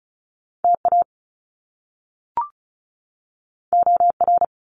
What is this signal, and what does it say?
Morse code, heard as a keyed tone, about 36 wpm, 700 Hz. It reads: TA OR
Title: Top Two Letter Combinations in Words 35wpm